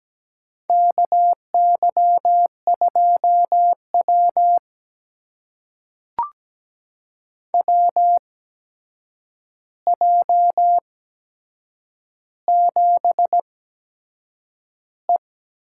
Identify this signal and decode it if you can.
Morse code, heard as a keyed tone, about 17 wpm, spaced as for 7 wpm, 700 Hz. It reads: KY2W WJ7E
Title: USA Call Signs - 4 Characters - 8x Character Spacing - 17wpm